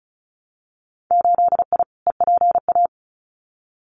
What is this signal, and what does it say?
Morse code, heard as a keyed tone, about 34 wpm, 700 Hz. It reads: 8I EPA
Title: Winter Field Day Exchanges Encore 35wpm